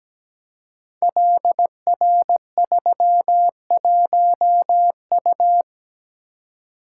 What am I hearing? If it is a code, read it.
Morse code, heard as a keyed tone, about 17 wpm, 700 Hz. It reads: LR31U
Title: Code Groups - 5 Letter:Numbers - 17wpm